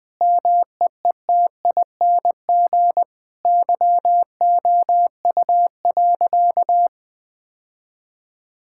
Morse code, heard as a keyed tone, about 20 wpm, 700 Hz. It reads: MEETING YOU.
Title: Sentences with 7-Word Phrases - Spoken Beginning - 20wpm